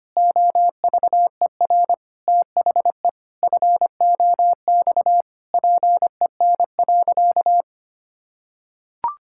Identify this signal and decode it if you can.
Morse code, heard as a keyed tone, about 25 wpm, 700 Hz. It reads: OVER THE FOX PEN.
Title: Pangram Challenge v2 - 25wpm